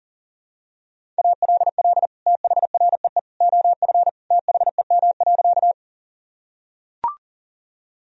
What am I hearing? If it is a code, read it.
Morse code, heard as a keyed tone, about 37 wpm, 700 Hz. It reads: ALL THREE OF THEM.
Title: Sentences from Top 1000 Words - Review Part 1 - 40wpm